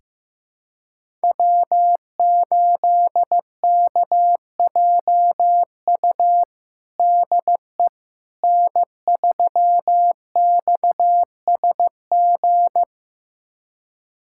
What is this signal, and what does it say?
Morse code, heard as a keyed tone, about 15 wpm, 700 Hz. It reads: W8KJU DE N3XSG
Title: USA Callsign DE Callsign - 5 Characters - 15wpm